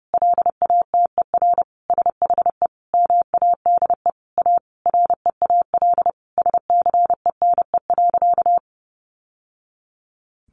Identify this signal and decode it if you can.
Morse code, heard as a keyed tone, about 30 wpm, 700 Hz. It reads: LATER SHE MADE A REAL SCENE.
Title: Sentences from Top 1100 Words 30wpm